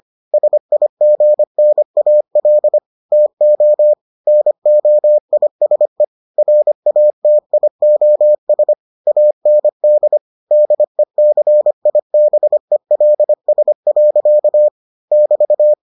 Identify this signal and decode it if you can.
Morse code, heard as a keyed tone, about 25 wpm, 600 Hz. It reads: SIGNAL TO NOISE RATIOS AND DECIBELS. <BT>